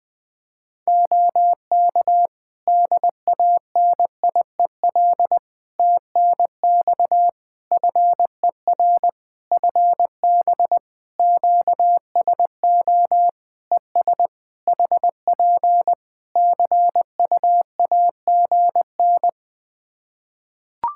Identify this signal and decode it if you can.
Morse code, heard as a keyed tone, about 20 wpm, 700 Hz. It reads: OK DANIEL TNX FER FB QSO ES HP CUAGN
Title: K1USN SST QSO Practice 20wpm